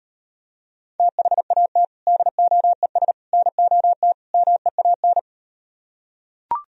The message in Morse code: THAT DOES NOT MEAN

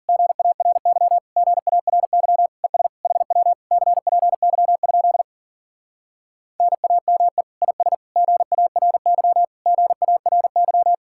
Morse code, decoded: GARY GARY ES HW CPY? NAME IS GARY GARY

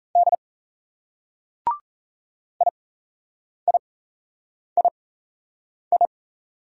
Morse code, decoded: D IIII

